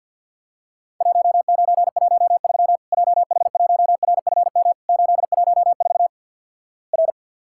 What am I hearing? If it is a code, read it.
Morse code, heard as a keyed tone, about 50 wpm, 700 Hz. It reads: 1913 JH1RFM 814 R
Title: Worked All Europe DX Contest - QTC Exchange with Repeats - 50wpm